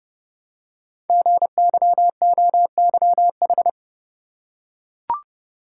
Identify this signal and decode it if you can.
Morse code, heard as a keyed tone, about 28 wpm, 700 Hz. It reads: GYOYH
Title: Code Groups - 5 Letter:Numbers - 30wpm